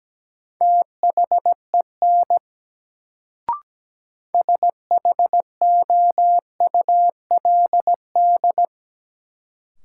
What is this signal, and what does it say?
Morse code, heard as a keyed tone, about 17 wpm, 700 Hz. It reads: THEN SHOULD
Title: Sets of 2 Words - New Word Drill - 17wpm